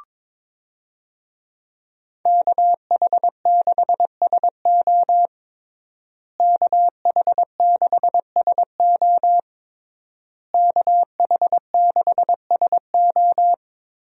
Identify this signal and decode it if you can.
Morse code, heard as a keyed tone, about 22 wpm, 700 Hz. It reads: KH6SO KH6SO KH6SO
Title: USA Call Signs - 5 Characters - Repeated 3x - 22wpm